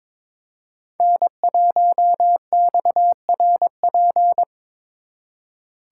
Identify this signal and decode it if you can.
Morse code, heard as a keyed tone, about 26 wpm, 700 Hz. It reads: N1XRP
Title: Speed Racing - USA 1x3 Call Signs 22wpm